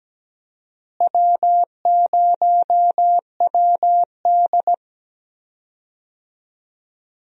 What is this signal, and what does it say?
Morse code, heard as a keyed tone, about 17 wpm, 700 Hz. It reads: W0WD
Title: USA Call Signs - 4 Characters - Repeated 2x - 17wpm